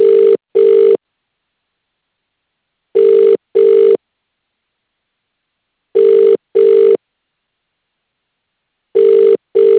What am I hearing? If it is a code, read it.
Ringback tone (UK), 400 + 450 Hz.